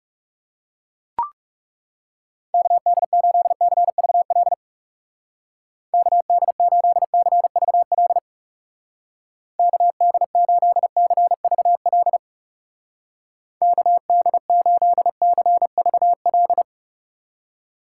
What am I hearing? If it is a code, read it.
Morse code, heard as a keyed tone, about 36 wpm, 700 Hz. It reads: KD8CVL KD8CVL KD8CVL KD8CVL
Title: Speed Racing - USA 2x3 Call Signs 30wpm